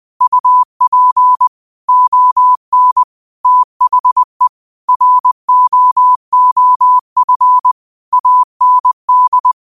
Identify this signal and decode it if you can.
Morse code, heard as a keyed tone, about 20 wpm, 1000 Hz. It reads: UP ON THE ROOF AND
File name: qotdfiles / 20wpm